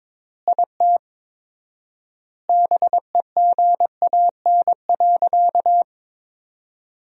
Morse code, heard as a keyed tone, about 22 wpm, 700 Hz. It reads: IT BEGAN.
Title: 4x Word-Spacing - Sentences from Top 300 Words - 22wpm